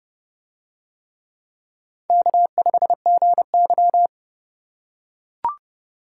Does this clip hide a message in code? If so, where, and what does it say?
Yes: Morse code from 2.1 to 4.1 s, heard as a keyed tone, about 29 wpm, 700 Hz. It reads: K5GY